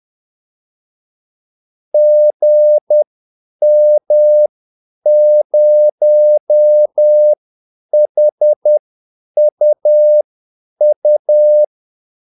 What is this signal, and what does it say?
Morse code, heard as a keyed tone, about 10 wpm, spaced as for 8 wpm, 600 Hz. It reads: GM0HUU
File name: Callsigns-10-8wpm